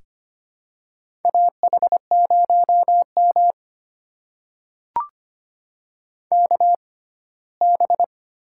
Morse code, heard as a keyed tone, about 25 wpm, spaced as for 17 wpm, 700 Hz. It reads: AH0M KB
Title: USA Call Signs - 6x Character Spacing - 25wpm